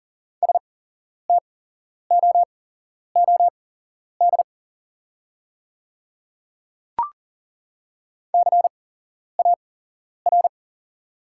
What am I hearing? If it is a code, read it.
Morse code, heard as a keyed tone, about 40 wpm, spaced as for 11 wpm, 690 Hz. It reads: STOOD CAR
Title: 8x Character Spacing - Top 500 Words - 40wpm